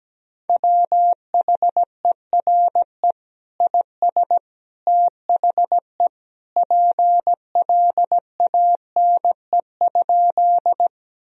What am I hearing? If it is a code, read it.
Morse code, heard as a keyed tone, about 17 wpm, 700 Hz. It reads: WHERE IS THE PLANE?
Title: Sentences from Top 1000 Words - Review Part 3 - 17wpm